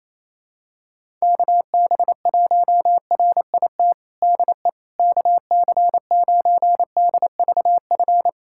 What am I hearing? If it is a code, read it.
Morse code, heard as a keyed tone, about 28 wpm, 700 Hz. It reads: KB1RIT DE KC9DVF
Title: USA Callsign DE Callsign - 6 Characters - 28wpm